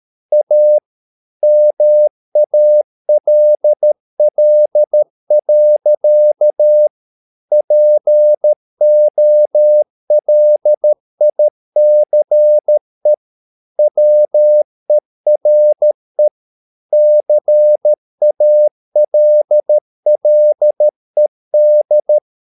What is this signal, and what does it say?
Morse code, heard as a keyed tone, about 13 wpm, 600 Hz. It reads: A MALL. POLICE WERE CALLED